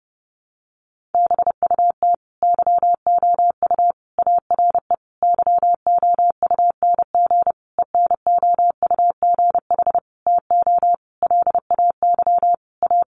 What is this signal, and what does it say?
Morse code, heard as a keyed tone, about 30 wpm, 700 Hz. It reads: BUT YOU ARE YOUNG ENOUGH TO LAY A
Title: Sentences from Top 1200 Words 30wpm